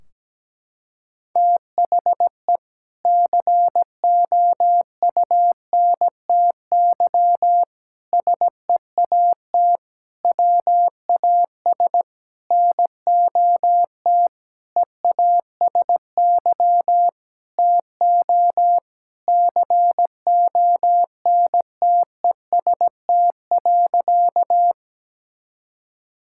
Morse code, heard as a keyed tone, about 17 wpm, 700 Hz. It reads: THE COUNTY SEAT WAS NOT EASY TO CONTEST.
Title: Sentences from Top 1600 Words 17wpm